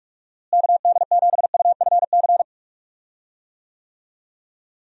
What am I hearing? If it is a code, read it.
Morse code, heard as a keyed tone, about 45 wpm, 700 Hz. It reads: KD7UFC
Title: USA Call Signs - 6 Characters - Repeated 3x - 45wpm